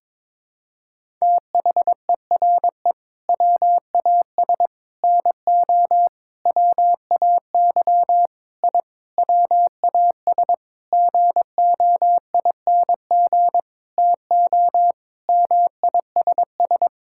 Morse code, heard as a keyed tone, about 22 wpm, 700 Hz. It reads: THERE WAS NO WAY I WAS GOING TO MISS
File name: Sentences with 7-Word Phrases - 22wpm